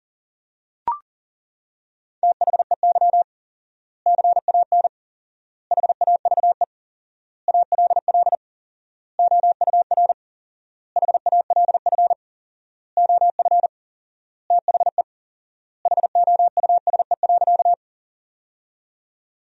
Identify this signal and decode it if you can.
Morse code, heard as a keyed tone, about 40 wpm, 700 Hz. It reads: THEY CAN HAVE ALL OUR HALF OF THE HOUSE.
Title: 4x Word-Spacing - Sentences from Top 400 Words - 40wpm